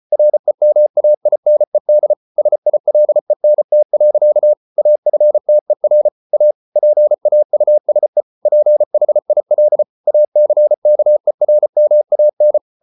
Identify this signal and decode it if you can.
Morse code, heard as a keyed tone, about 34 wpm, 600 Hz. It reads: REMAINED SILENT. AFTER A PAUSE PHIL ACKERMAN